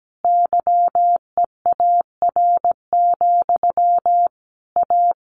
Morse code, heard as a keyed tone, about 17 wpm, 700 Hz. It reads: YEAR, A
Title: A Sound Of Thunder 17wpm